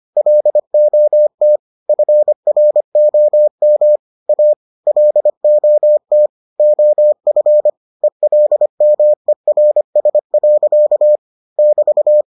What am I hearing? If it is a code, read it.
Morse code, heard as a keyed tone, about 25 wpm, 600 Hz. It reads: LOT FROM A LOT OF ELMERS. <BT>